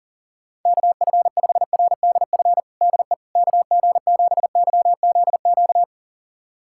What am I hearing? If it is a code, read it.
Morse code, heard as a keyed tone, about 40 wpm, 700 Hz. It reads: KF5RDF DE KG7YZQ
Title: USA Callsign DE Callsign - 6 Characters - 40wpm